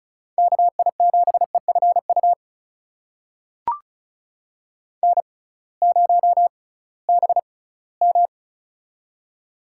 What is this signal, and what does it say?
Morse code, heard as a keyed tone, about 36 wpm, spaced as for 20 wpm, 700 Hz. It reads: KI7EFU N0BM
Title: USA Call Signs - 6x Character Spacing - 35wpm